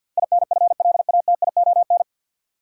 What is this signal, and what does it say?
Morse code, heard as a keyed tone, about 53 wpm, 700 Hz. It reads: INFLATION
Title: Sentences from Top 1100 Words - New Word Drill - 50wpm